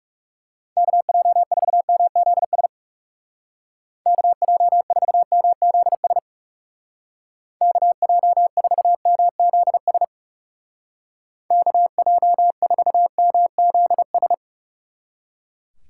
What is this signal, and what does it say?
Morse code, heard as a keyed tone, about 36 wpm, 700 Hz. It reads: KJ4MZS KJ4MZS KJ4MZS KJ4MZS
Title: Speed Racing - USA Call Signs 30wpm